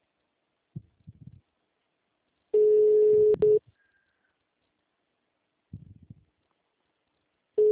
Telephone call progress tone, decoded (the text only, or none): ringback tone